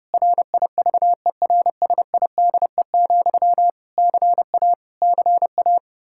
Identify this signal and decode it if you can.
Morse code, heard as a keyed tone, about 30 wpm, 700 Hz. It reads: RIVERSIDE, CA CA
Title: Practice QSO - Exchange 1 - Beginning - 30wpm